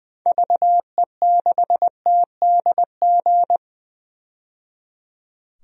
Morse code, heard as a keyed tone, about 20 wpm, 700 Hz. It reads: VE6TDG
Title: DXCC-100 Call Signs - 6 Characters - 20wpm